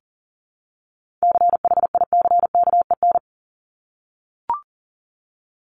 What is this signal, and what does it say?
Morse code, heard as a keyed tone, about 40 wpm, 700 Hz. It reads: CHICKEN
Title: Multi-letter Phonemes 40wpm